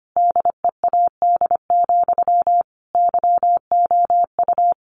Morse code, heard as a keyed tone, about 25 wpm, 700 Hz. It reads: DEAD, YOU
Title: A Sound Of Thunder 25wpm